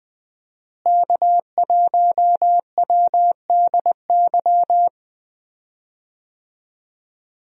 Morse code, heard as a keyed tone, about 20 wpm, 700 Hz. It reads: K1WDY
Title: USA Call Signs - 5 Characters - Repeated 2x - 20wpm